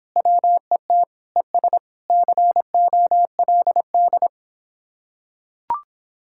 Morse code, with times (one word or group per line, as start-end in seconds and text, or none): 0.2-1.0 WET
1.4-1.8 ES
2.1-4.3 COLD